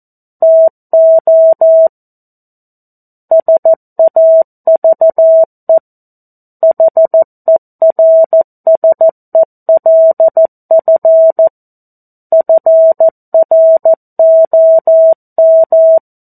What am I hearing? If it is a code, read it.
Morse code, heard as a keyed tone, about 14 wpm, 650 Hz. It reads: TO SAVE HERSELF FROM
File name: 02-Chapter-02 14 WPM